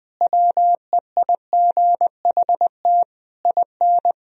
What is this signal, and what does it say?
Morse code, heard as a keyed tone, about 20 wpm, 700 Hz. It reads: WEIGHT IN
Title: Sentences from Top 1000 Words - Review Part 3 - 20wpm